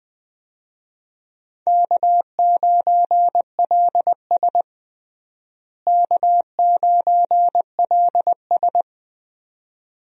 Morse code, heard as a keyed tone, about 21 wpm, 700 Hz. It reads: K9LS K9LS
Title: USA Call Signs - 4 Characters - Repeated 2x - 20wpm